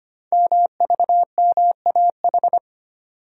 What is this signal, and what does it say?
Morse code, heard as a keyed tone, about 25 wpm, 700 Hz. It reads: MVMAH
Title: Code Groups - 5 Letters - 25wpm